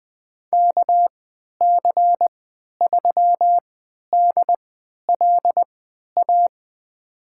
Morse code, heard as a keyed tone, about 20 wpm, spaced as for 11 wpm, 700 Hz. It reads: KC3DLA
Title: USA Call Signs - 3x Character Spacing - 20wpm